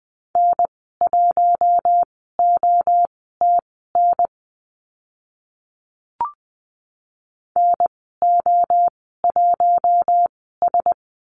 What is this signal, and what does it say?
Morse code, heard as a keyed tone, about 20 wpm, spaced as for 14 wpm, 700 Hz. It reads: N1OTN NO1S
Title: 042 Lesson 8g - Number 1 - Call Signs 20wpm